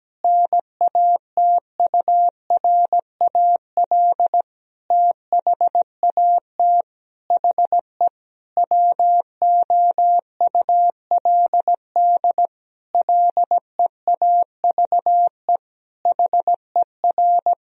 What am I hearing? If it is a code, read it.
Morse code, heard as a keyed tone, about 17 wpm, 700 Hz. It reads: NATURAL THAT HE WOULD LEAVE HER
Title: Sentences from Top 1000 Words - Review Part 4 - 17wpm